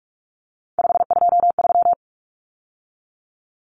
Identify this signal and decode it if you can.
Morse code, heard as a keyed tone, about 44 wpm, 700 Hz. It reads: 523
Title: Numbers - 3-Digit 45wpm